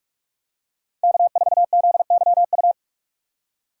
Morse code, heard as a keyed tone, about 44 wpm, 700 Hz. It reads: K4ZYU
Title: USA Call Signs - 5 Characters - 6x Character Spacing - 45wpm